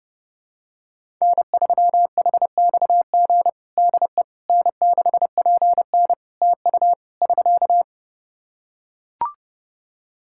Morse code, heard as a keyed tone, about 30 wpm, 700 Hz. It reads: N3HXG DE N6PN TU <SK>
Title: K1USN SST QSO Practice 30wpm